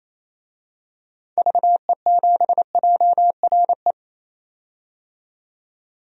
Morse code, heard as a keyed tone, about 28 wpm, 700 Hz. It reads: VE7JRE
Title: DXCC-100 Call Signs - 6 Characters - 28wpm